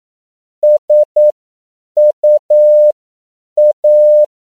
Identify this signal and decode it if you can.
Morse code, heard as a keyed tone, about 9 wpm, spaced as for 7 wpm, 600 Hz. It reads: SUA
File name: CW / 10PPM